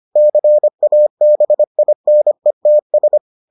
Morse code, heard as a keyed tone, about 25 wpm, 600 Hz. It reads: CABINETS